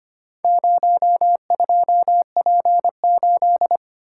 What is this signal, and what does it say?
Morse code, heard as a keyed tone, about 25 wpm, 700 Hz. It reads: 02P8
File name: practice / 25wpm / Four Letter-Number Combinations - 25wpm